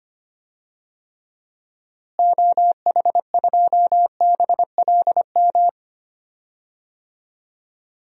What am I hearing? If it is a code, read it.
Morse code, heard as a keyed tone, about 25 wpm, 700 Hz. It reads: OH2BLM
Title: DXCC-100 Call Signs - 6 Characters - Repeated 2x - 25wpm